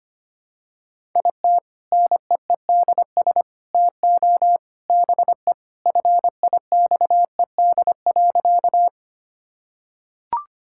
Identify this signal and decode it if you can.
Morse code, heard as a keyed tone, about 25 wpm, 700 Hz. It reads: IT NEEDS TO BE FIXED.
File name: practice / 25wpm / Sentences with 3-Word Phrases - 25wpm